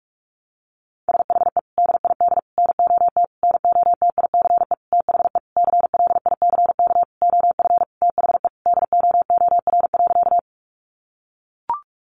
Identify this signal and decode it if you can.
Morse code, heard as a keyed tone, about 44 wpm, 700 Hz. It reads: SHE DID NOT NOTICE THE CLICK OF THE DOOR.
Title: Sentences from Top 1100 Words 45wpm